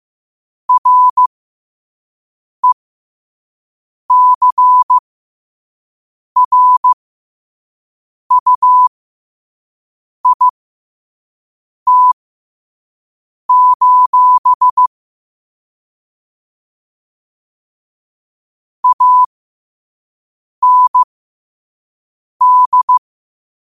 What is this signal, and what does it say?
Morse code, heard as a keyed tone, about 15 wpm, spaced as for 5 wpm, 1000 Hz. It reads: RECRUIT: AND
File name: qotdfiles / 5wpm